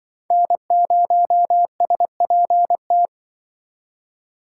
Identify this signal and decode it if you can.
Morse code, heard as a keyed tone, about 25 wpm, 700 Hz. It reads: N0SPT